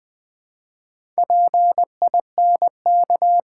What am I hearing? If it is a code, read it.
Morse code, heard as a keyed tone, about 20 wpm, 700 Hz. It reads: PINK
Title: Build A Word - 2 Letters At A Time - 4-Letter Words - 20wpm